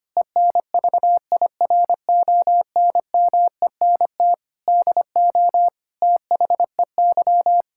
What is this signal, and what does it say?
Morse code, heard as a keyed tone, about 25 wpm, 700 Hz. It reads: ENVIRONMENT DO THEY
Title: Sentences from Top 1000 Words - Review Part 5 - 25wpm